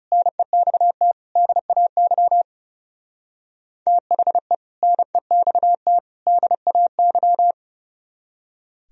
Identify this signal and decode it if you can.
Morse code, heard as a keyed tone, about 36 wpm, 700 Hz. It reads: NEXT DAY THE NEXT DAY
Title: Speed-Racing - Sets of 3 Words 30wpm